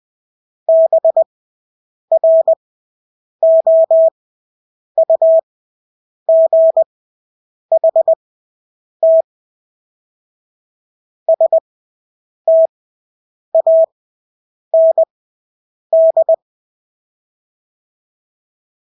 Morse code, heard as a keyed tone, about 20 wpm, spaced as for 8 wpm, 650 Hz. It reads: BROUGHT STAND
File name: Engelske ord 20/8 wpm